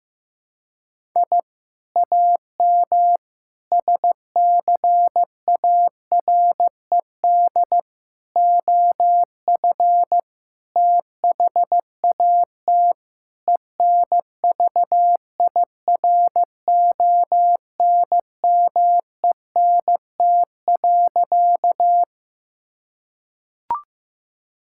Morse code, heard as a keyed tone, about 16 wpm, 700 Hz. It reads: I AM SCARED OF THAT ENVIRONMENT.
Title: Sentences from Top 1000 Words - Review Part 5 - 15wpm